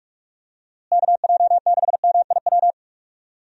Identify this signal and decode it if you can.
Morse code, heard as a keyed tone, about 42 wpm, 700 Hz. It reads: KJ6MIW
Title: USA Call Signs - 6 Characters - Repeated 2x - 45wpm